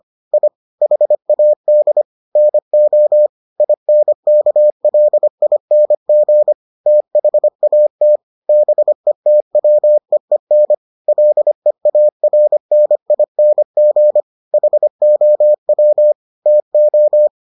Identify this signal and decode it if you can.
Morse code, heard as a keyed tone, about 25 wpm, 600 Hz. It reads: I HAD NO INKLING THAT BETWEEN LEARNING HOW TO